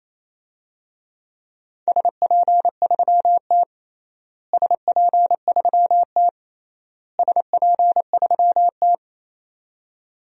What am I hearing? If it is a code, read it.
Morse code, heard as a keyed tone, about 28 wpm, 700 Hz. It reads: SP3T SP3T SP3T
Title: DXCC-100 Call Signs - 4 Characters - Repeated 3x - 28wpm